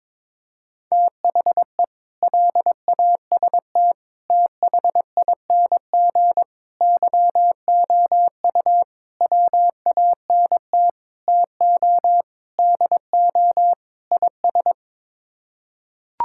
Morse code, heard as a keyed tone, about 22 wpm, 700 Hz. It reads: THE LAST THING YOU WANT TO DO IS
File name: practice / 22wpm / Sets of 8 Words - 22wpm